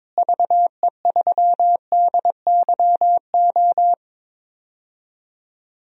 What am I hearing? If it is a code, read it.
Morse code, heard as a keyed tone, about 21 wpm, 700 Hz. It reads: VE3DYO
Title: DXCC-100 Call Signs - 6 Characters - Repeated 3x - 22wpm